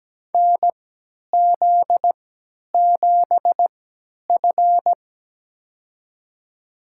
Morse code, heard as a keyed tone, about 17 wpm, spaced as for 10 wpm, 700 Hz. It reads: NZ7F
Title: USA Call Signs - 4 Characters - 3x Character Spacing - 17wpm